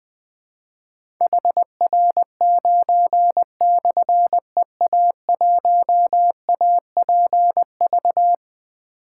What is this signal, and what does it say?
Morse code, heard as a keyed tone, about 21 wpm, 700 Hz. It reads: HR9/EA1APV
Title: Difficult Call Signs 20wpm